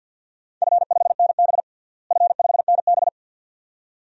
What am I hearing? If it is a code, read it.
Morse code, heard as a keyed tone, about 52 wpm, 700 Hz. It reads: F5NB F5NB
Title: DXCC-100 Call Signs - 4 Characters - Repeated 2x - 50wpm